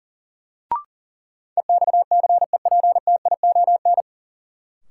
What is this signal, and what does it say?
Morse code, heard as a keyed tone, about 40 wpm, 700 Hz. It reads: EXCEPTION
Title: Sets of 4 Words - New Word Drill - 40wpm